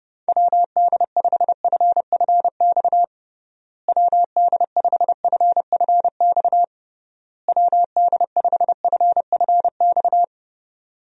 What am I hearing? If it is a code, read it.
Morse code, heard as a keyed tone, about 32 wpm, 700 Hz. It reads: WD5FFX WD5FFX WD5FFX
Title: USA Call Signs - 6 Characters - Repeated 3x - 30wpm